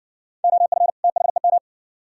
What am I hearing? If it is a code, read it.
Morse code, heard as a keyed tone, about 56 wpm, 690 Hz. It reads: OF THEM